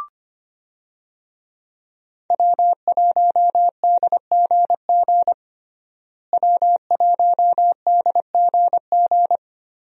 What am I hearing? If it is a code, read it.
Morse code, heard as a keyed tone, about 25 wpm, 700 Hz. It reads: W1DGG W1DGG
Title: USA Call Signs - 5 Characters - Repeated 3x - 25wpm